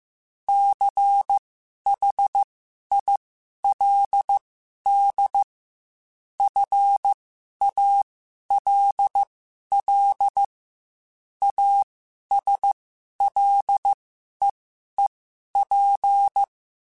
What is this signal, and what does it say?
Morse code, heard as a keyed tone, about 15 wpm, spaced as for 11 wpm, 800 Hz. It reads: CHILD FALL ASLEEP
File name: Quotes / Quotes-mp3s / ChildAsleep @13